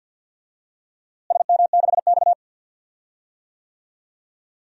Morse code, heard as a keyed tone, about 50 wpm, 700 Hz. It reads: SM6X
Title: DXCC-100 Call Signs - 4 Characters - Repeated 2x - 50wpm